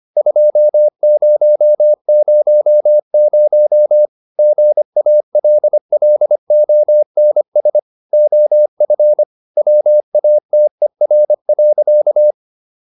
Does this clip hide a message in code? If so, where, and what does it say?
Yes: Morse code from 0.2 to 12.3 s, heard as a keyed tone, about 25 wpm, 600 Hz. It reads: 2000 GALLONS OF WATER.